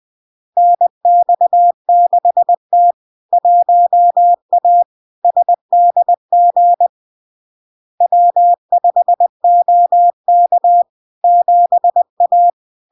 Morse code, heard as a keyed tone, about 20 wpm, 700 Hz. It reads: NX6T 1A SDG W5OK 7A
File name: LARK-FD-2022-Log-in-CW-20wpm-700-Hz